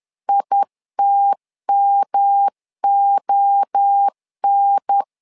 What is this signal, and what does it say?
Morse code, heard as a keyed tone, about 10 wpm, 800 Hz. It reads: ITMON
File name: LEZIONI / 4 Q1S 10wpm 2133